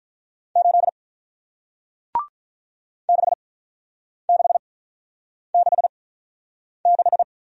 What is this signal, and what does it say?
Morse code, heard as a keyed tone, about 48 wpm, 700 Hz. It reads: 8 6 6 6 6